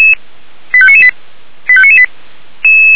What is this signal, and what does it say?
Five-tone selective call (ZVEI), heard as quick runs of five tones, tones of about 70 ms. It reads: 85998 85998